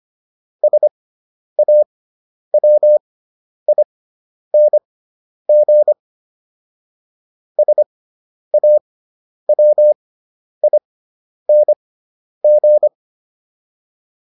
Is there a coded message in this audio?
Yes: Morse code from 0.6 to 12.9 s, heard as a keyed tone, about 25 wpm, spaced as for 10 wpm, 600 Hz. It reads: SAWING SAWING